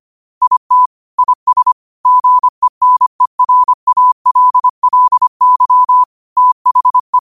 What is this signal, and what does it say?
Morse code, heard as a keyed tone, about 25 wpm, 1000 Hz. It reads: IT IS GENERALLY THE
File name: qotdfiles / 25wpm